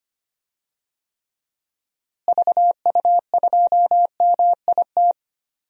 Morse code, heard as a keyed tone, about 25 wpm, 700 Hz. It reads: VU2MIT
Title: DXCC-100 Call Signs - 6 Characters - Repeated 3x - 25wpm